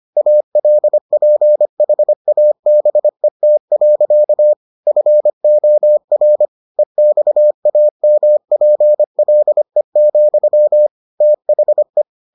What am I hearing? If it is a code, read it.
Morse code, heard as a keyed tone, about 25 wpm, 600 Hz. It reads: ALPHABET. FOR EXAMPLE, THE